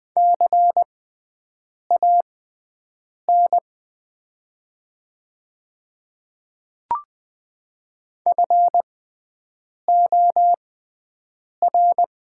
Morse code, heard as a keyed tone, about 20 wpm, spaced as for 7 wpm, 700 Hz. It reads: CAN FOR
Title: 6x Character Spacing - Top 200 Words - 20wpm